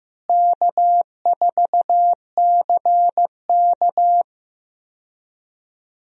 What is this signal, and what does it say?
Morse code, heard as a keyed tone, about 15 wpm, 700 Hz. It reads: K4CK
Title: USA Call Signs - 4 Characters - Repeated 2x - 15wpm